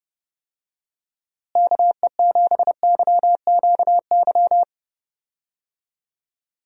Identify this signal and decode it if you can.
Morse code, heard as a keyed tone, about 30 wpm, 700 Hz. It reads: KE7YQY
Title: USA Call Signs - 6 Characters - Repeated 3x - 30wpm